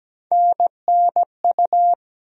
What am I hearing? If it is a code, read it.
Morse code, heard as a keyed tone, about 17 wpm, 700 Hz. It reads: NNU
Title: USA Call Sign Postfixes - 3 Letters 17wpm